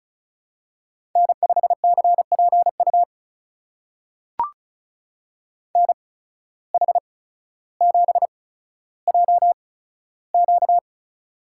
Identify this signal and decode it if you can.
Morse code, heard as a keyed tone, about 36 wpm, spaced as for 15 wpm, 700 Hz. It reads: N5CPU NH7JQ
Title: USA Call Signs - 8x Character Spacing - 35wpm